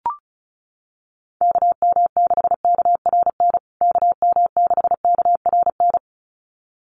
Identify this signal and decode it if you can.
Morse code, heard as a keyed tone, about 34 wpm, 700 Hz. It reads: KM6KRN KM6KRN
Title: USA Call Signs Repeated 2x 35wpm